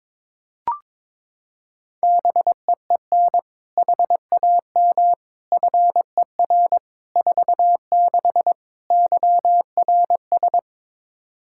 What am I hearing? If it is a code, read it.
Morse code, heard as a keyed tone, about 21 wpm, 700 Hz. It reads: BEEN HAM FER 46 YRS
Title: Practice QSO - Exchange 3 - More Info - 22wpm